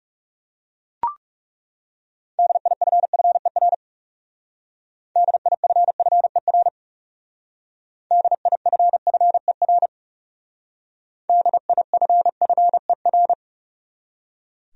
Morse code, heard as a keyed tone, about 38 wpm, 700 Hz. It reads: DIFFER DIFFER DIFFER DIFFER